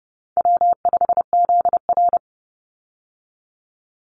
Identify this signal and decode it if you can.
Morse code, heard as a keyed tone, about 30 wpm, 700 Hz. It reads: W5ZR
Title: Easy Call Signs Repeated 3x - 30wpm